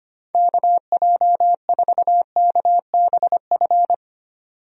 Morse code, heard as a keyed tone, about 25 wpm, 700 Hz. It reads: KJ4KBF
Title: USA Call Signs - 6 Characters - 25wpm